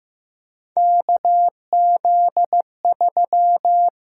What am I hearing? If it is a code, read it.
Morse code, heard as a keyed tone, about 15 wpm, 700 Hz. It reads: KZ3
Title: USA Call Signs - 4 Characters - 6x Character Spacing - 15wpm